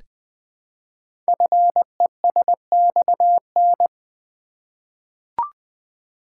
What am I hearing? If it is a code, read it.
Morse code, heard as a keyed tone, about 20 wpm, 700 Hz. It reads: FESXN
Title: Code Groups - 5 Letters - 20wpm